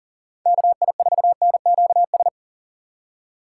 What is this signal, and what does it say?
Morse code, heard as a keyed tone, about 37 wpm, 700 Hz. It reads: KI4NQS
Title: USA Call Signs - 2x Character Spacing - 40wpm